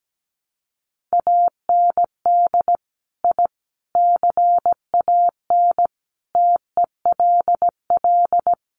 Morse code, heard as a keyed tone, about 17 wpm, 700 Hz. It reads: AND I CAN TELL
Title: Sets of 5 Words Encore2 17wpm